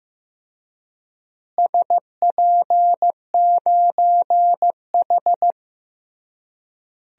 Morse code, heard as a keyed tone, about 16 wpm, 700 Hz. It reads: SP9H
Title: DXCC-100 Call Signs - 4 Characters - Repeated 3x - 15wpm